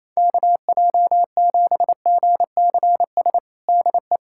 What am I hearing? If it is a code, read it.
Morse code, heard as a keyed tone, about 28 wpm, 700 Hz. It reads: KJ7GCS DE
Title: USA Callsign DE Callsign 28wpm